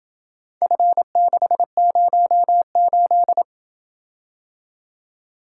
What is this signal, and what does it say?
Morse code, heard as a keyed tone, about 27 wpm, 700 Hz. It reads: F608